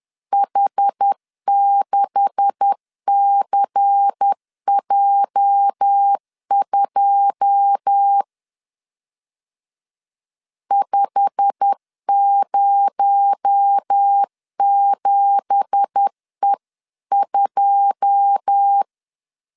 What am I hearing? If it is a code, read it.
Morse code, heard as a keyed tone, about 10 wpm, 800 Hz. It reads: H6CJ2 507E2